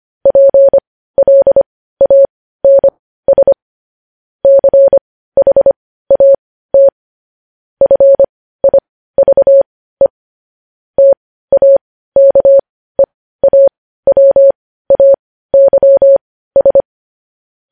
Morse code, heard as a keyed tone, about 25 wpm, spaced as for 15 wpm, 550 Hz. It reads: PLANS CHAT FIVE TAKEAWAYS